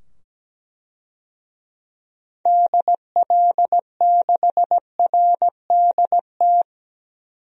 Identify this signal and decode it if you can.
Morse code, heard as a keyed tone, about 17 wpm, 700 Hz. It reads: DL6RDT
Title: DXCC-100 Call Signs - 6 Characters - 17wpm